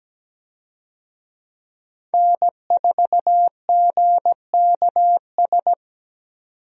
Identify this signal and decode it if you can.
Morse code, heard as a keyed tone, about 17 wpm, 700 Hz. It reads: N4GKS
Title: USA Call Signs - 5 Characters - 17wpm